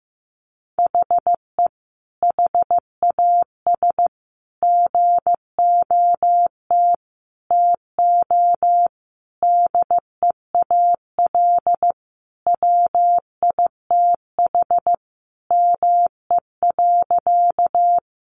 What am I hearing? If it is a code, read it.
Morse code, heard as a keyed tone, about 15 wpm, 700 Hz. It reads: HE HAS GOT TO DEAL WITH ME.
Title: Sentences from Top 800 Words 15wpm